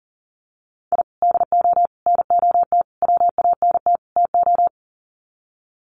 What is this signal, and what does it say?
Morse code, heard as a keyed tone, about 40 wpm, 700 Hz. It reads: I DO NOT WANT TO
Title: Sets of 5 Words Encore2 40wpm